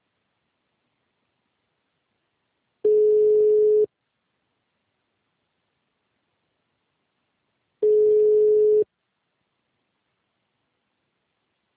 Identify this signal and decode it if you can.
Ringback tone (Europe, 425 Hz).